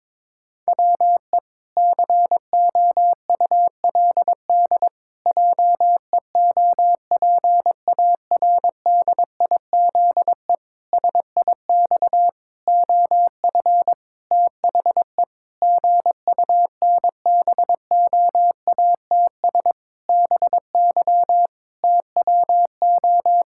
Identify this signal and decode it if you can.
Morse code, heard as a keyed tone, about 22 wpm, 700 Hz. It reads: WE COULD JEOPARDIZE SIX OF THE GUNBOATS BY TWO
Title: Pangram Challenge v2 - 22wpm